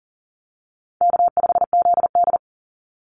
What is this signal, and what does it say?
Morse code, heard as a keyed tone, about 40 wpm, 700 Hz. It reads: K5ZD
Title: Easy Call Signs Repeated 3x - 40wpm